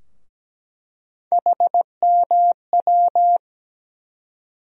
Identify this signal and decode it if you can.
Morse code, heard as a keyed tone, about 17 wpm, 700 Hz. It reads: HMW
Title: USA Call Sign Postfixes - 3 Letters 17wpm